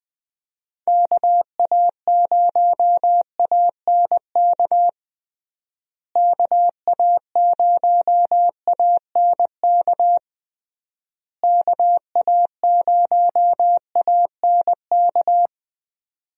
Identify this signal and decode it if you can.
Morse code, heard as a keyed tone, about 19 wpm, 700 Hz. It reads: KA0ANK KA0ANK KA0ANK
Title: USA Call Signs - 6 Characters - Repeated 3x - 20wpm